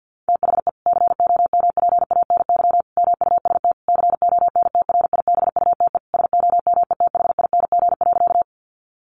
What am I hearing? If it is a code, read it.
Morse code, heard as a keyed tone, about 50 wpm, 700 Hz. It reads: THE COMPANY MUST CONTRIBUTE SOMETHING.
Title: Sentences from Top 1400 Words 50wpm